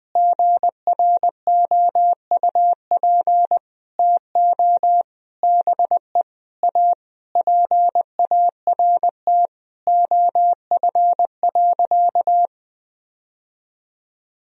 Morse code, heard as a keyed tone, about 20 wpm, 700 Hz. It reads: GROUP TO BE A PART OF.
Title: Sentences from Top 1000 Words - Review Part 1 - 20wpm